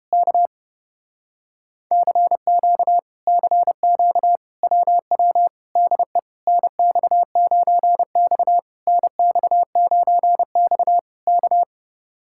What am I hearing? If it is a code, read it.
Morse code, heard as a keyed tone, about 30 wpm, 700 Hz. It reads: K CQ CQ WW DE NX9X NX9X K